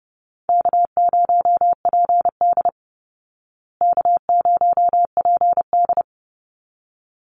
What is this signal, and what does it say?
Morse code, heard as a keyed tone, about 30 wpm, 700 Hz. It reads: K0PD K0PD
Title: Easy Call Signs Repeated 2x - 30wpm